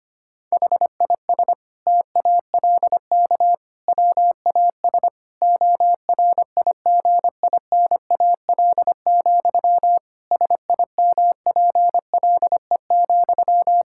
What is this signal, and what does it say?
Morse code, heard as a keyed tone, about 25 wpm, 700 Hz. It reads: HIS TALK WAS ORIGINAL, SIMPLE,
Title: Sentences from Top 1000 Words - Review Part 6 - 25wpm